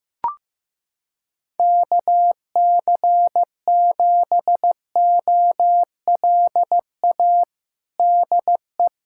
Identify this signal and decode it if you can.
Morse code, heard as a keyed tone, about 16 wpm, 700 Hz. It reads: KC7OLA DE
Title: USA Callsign DE Callsign 15wpm